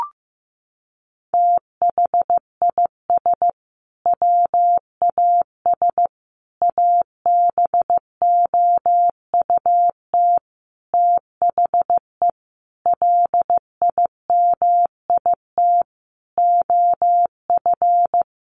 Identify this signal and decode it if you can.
Morse code, heard as a keyed tone, about 15 wpm, 700 Hz. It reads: THIS WAS ABOUT THE LIMIT OF
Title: Sentences from Top 1500 Words 15wpm